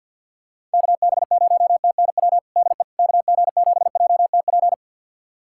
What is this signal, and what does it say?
Morse code, heard as a keyed tone, about 51 wpm, 700 Hz. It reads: KB0TNW DE KG7JTP
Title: USA Callsign DE Callsign - 6 Characters - 50wpm